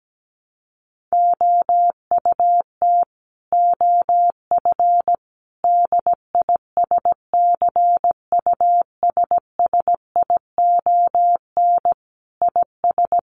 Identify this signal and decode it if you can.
Morse code, heard as a keyed tone, about 17 wpm, 700 Hz. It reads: OUT OF DISCUSSION IS
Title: Sentences from Top 1100 Words 17wpm